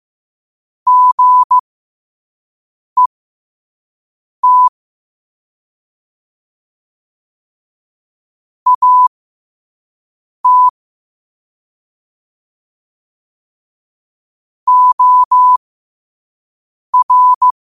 Morse code, heard as a keyed tone, about 15 wpm, spaced as for 5 wpm, 1000 Hz. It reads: GET AT OR
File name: qotdfiles / 5wpm